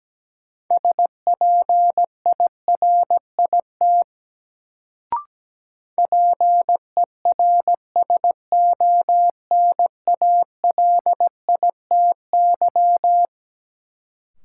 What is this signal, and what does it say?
Morse code, heard as a keyed tone, about 17 wpm, 700 Hz. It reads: SPIRIT PERSONALITY
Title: Sentences from Top 1400 Words - New Word Drill Alt - 17wpm